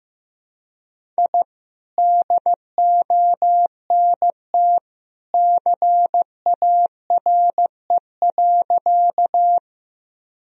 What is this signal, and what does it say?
Morse code, heard as a keyed tone, about 15 wpm, 700 Hz. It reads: I DONT CARE.
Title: Sentences from Top 1000 Words - Review Part 1 - 15wpm